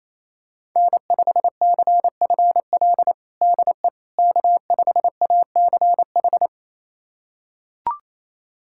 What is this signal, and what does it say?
Morse code, heard as a keyed tone, about 28 wpm, 700 Hz. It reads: N5CFL DE K5ACH
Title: USA Callsign DE Callsign - 5 Characters - 28wpm